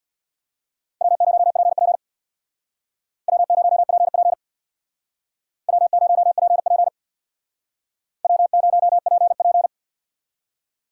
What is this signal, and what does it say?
Morse code, heard as a keyed tone, about 64 wpm, 700 Hz. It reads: W0PP W0PP W0PP W0PP